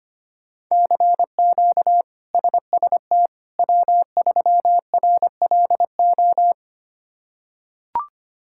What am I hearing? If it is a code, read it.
Morse code, heard as a keyed tone, about 25 wpm, 700 Hz. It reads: CQ SST W3RLO
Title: K1USN SST QSO Practice - 25wpm